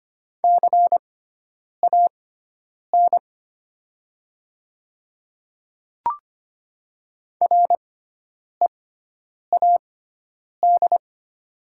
Morse code, heard as a keyed tone, about 25 wpm, spaced as for 9 wpm, 700 Hz. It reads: CAN READ
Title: 6x Character Spacing - Top 300 Words - 25wpm